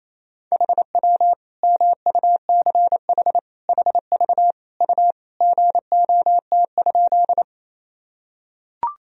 Morse code, heard as a keyed tone, about 28 wpm, 700 Hz. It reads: HW MUCH HV U GOT?
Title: Sentences from Top 300 Words and Abbreviations 28wpm